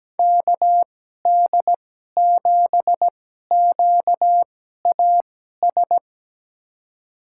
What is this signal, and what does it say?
Morse code, heard as a keyed tone, about 17 wpm, spaced as for 12 wpm, 700 Hz. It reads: KD7QAS
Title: USA Call Signs - 6 Characters - 2x Character Spacing - 17wpm